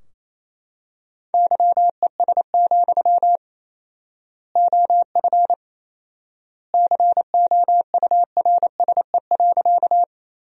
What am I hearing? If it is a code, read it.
Morse code, heard as a keyed tone, about 27 wpm, 700 Hz. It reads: YES, OF COURSE.
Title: 4x Word-Spacing - Sentences from Top 500 Words - 28wpm